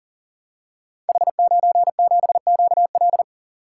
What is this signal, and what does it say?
Morse code, heard as a keyed tone, about 40 wpm, 700 Hz. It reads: H97QL
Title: Code Groups - 5 Letter:Numbers - 40wpm